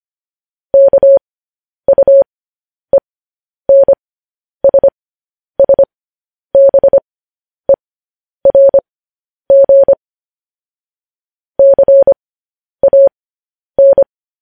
Morse code, heard as a keyed tone, about 25 wpm, spaced as for 10 wpm, 550 Hz. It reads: KUENSSBERG CAN
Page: news headlines in morse code at 10 wpm